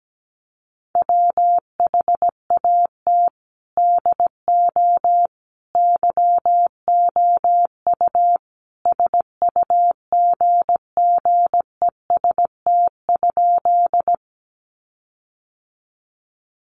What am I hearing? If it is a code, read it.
Morse code, heard as a keyed tone, about 17 wpm, 700 Hz. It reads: WHAT DO YOU SUGGEST?
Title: Sentences from Top 1300 Words 17wpm